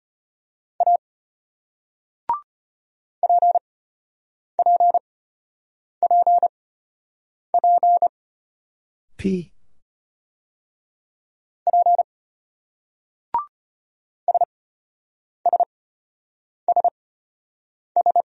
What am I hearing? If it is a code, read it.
Morse code, heard as a keyed tone, about 34 wpm, spaced as for 8 wpm, 700 Hz. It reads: A PPPP P SSSS